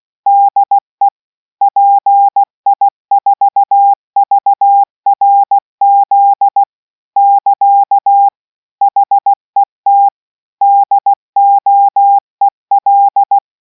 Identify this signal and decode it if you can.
Morse code, heard as a keyed tone, about 16 wpm, 800 Hz. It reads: DE PI4VRZ <KA> HET DOEL